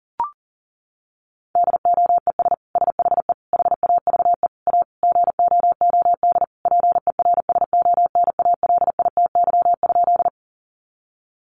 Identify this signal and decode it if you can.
Morse code, heard as a keyed tone, about 40 wpm, 700 Hz. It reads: DOES SHE HAVE A GOOD PERSONALITY?
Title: Sentences from Top 1400 Words 40wpm